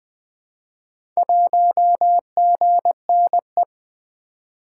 Morse code, heard as a keyed tone, about 20 wpm, 700 Hz. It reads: 1GNE
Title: Four Letter-Number Combinations - 20wpm